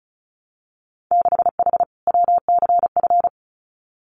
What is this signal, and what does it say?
Morse code, heard as a keyed tone, about 36 wpm, 700 Hz. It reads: 6H WCF
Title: Winter Field Day Exchanges Encore 35wpm